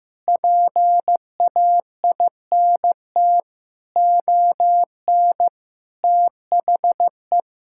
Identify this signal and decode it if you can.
Morse code, heard as a keyed tone, about 15 wpm, 700 Hz. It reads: PAINT ON THE
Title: Sentences from Top 1000 Words - Review Part 6 - 15wpm